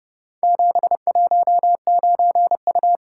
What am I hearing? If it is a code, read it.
Morse code, heard as a keyed tone, about 29 wpm, 700 Hz. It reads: 719U